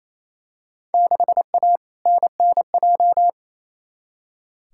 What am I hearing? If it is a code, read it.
Morse code, heard as a keyed tone, about 28 wpm, 700 Hz. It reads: 6A NNJ
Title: Field Day Exchanges Encore - 28wpm